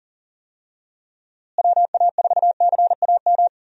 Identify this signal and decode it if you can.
Morse code, heard as a keyed tone, about 40 wpm, 700 Hz. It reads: WA4CAM
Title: USA Call Signs - 6 Characters - Repeated 2x - 40wpm